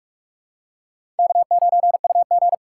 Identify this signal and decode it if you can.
Morse code, heard as a keyed tone, about 42 wpm, 700 Hz. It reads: K9UG
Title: USA Call Signs - 4 Characters - 6x Character Spacing - 45wpm